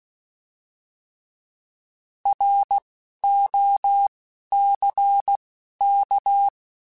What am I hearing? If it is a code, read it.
Morse code, heard as a keyed tone, about 16 wpm, spaced as for 11 wpm, 800 Hz. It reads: ROCK